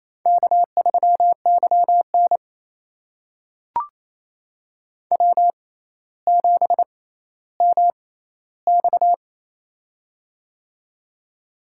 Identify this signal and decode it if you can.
Morse code, heard as a keyed tone, about 28 wpm, spaced as for 14 wpm, 700 Hz. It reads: K3YN W7MX
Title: USA Call Signs - 4 Characters - 6x Character Spacing - 28wpm